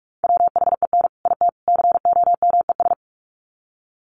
Morse code, heard as a keyed tone, about 44 wpm, 700 Hz. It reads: WHEN IT COMES
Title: Sets of 3 Words 45wpm